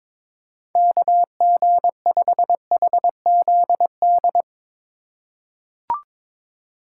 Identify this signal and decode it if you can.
Morse code, heard as a keyed tone, about 22 wpm, 700 Hz. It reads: KG5HZD
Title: USA Call Signs - 6 Characters - 4x Character Spacing - 22wpm